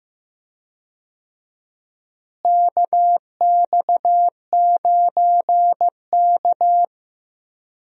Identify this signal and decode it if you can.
Morse code, heard as a keyed tone, about 15 wpm, 700 Hz. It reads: KX9K
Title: USA Call Signs - 4 Characters - Repeated 2x - 15wpm